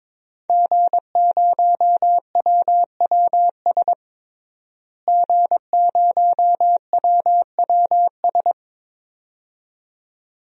Morse code, heard as a keyed tone, about 22 wpm, 700 Hz. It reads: G0WWS G0WWS
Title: DXCC-100 Call Signs - 5 Characters - Repeated 2x - 22wpm